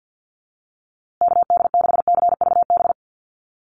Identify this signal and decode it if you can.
Morse code, heard as a keyed tone, about 50 wpm, 700 Hz. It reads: KD6CVB
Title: USA Call Signs Repeated 3x 50wpm